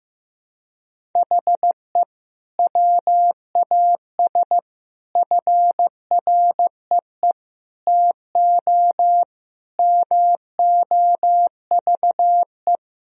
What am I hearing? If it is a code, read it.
Morse code, heard as a keyed tone, about 15 wpm, 700 Hz. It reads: HE WAS FREE TO MOVE
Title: Sentences from Top 1000 Words - Review Part 1 - 15wpm